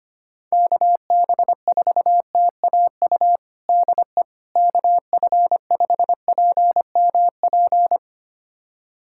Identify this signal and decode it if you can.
Morse code, heard as a keyed tone, about 25 wpm, 700 Hz. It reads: KB4TAU DE KF5PMP
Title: USA Callsign DE Callsign - 6 Characters - 25wpm